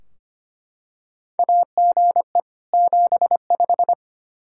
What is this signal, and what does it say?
Morse code, heard as a keyed tone, about 25 wpm, 700 Hz. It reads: AGE 75
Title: K1USN SST QSO Practice 25wpm